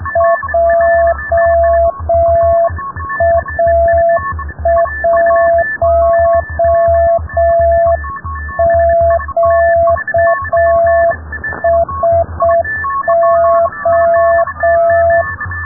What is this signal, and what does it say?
Morse code, heard as a keyed tone, about 6 wpm, 670 Hz. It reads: JA1QSO
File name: rubberst01 30 noise